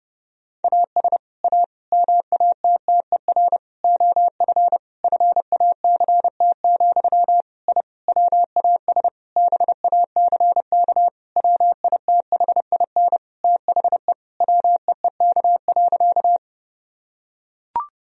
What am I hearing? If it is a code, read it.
Morse code, heard as a keyed tone, about 30 wpm, 700 Hz. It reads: AS A MATTER OF FACT, I WAS BACK WITHIN THE WEEK.
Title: Sentences from Top 1000 Words - Review Part 4 - 30wpm